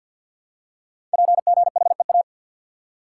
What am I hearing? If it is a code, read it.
Morse code, heard as a keyed tone, about 53 wpm, 700 Hz. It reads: PGHEA
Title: Code Groups - 5 Letters - 50wpm